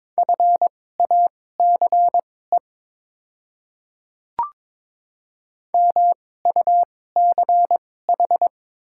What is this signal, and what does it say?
Morse code, heard as a keyed tone, about 22 wpm, spaced as for 16 wpm, 700 Hz. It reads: FACE MUCH
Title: 2x Character Spacing - Top 500 Words - 22wpm